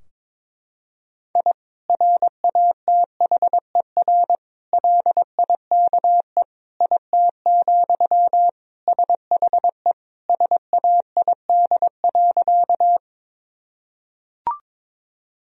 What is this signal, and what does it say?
Morse code, heard as a keyed tone, about 21 wpm, 700 Hz. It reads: I RATHER LIKE IT, SHE SAID.
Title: Sentences from Top 1000 Words - Review Part 4 - 22wpm